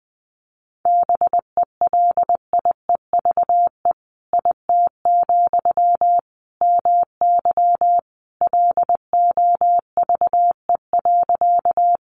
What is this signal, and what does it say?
Morse code, heard as a keyed tone, about 20 wpm, 700 Hz. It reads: BELIEVE IT, MY LOVE.
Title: Sentences from Top 1000 Words 20wpm